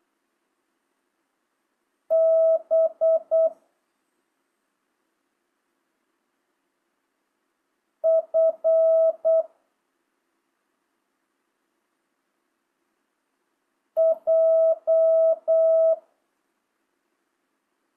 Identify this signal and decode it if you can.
Morse code, heard as a keyed tone, about 8 wpm, 650 Hz. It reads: B F J